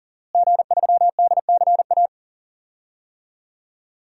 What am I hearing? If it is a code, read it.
Morse code, heard as a keyed tone, about 39 wpm, 700 Hz. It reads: G3DCA
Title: DXCC-100 Call Signs - 5 Characters - Repeated 3x - 40wpm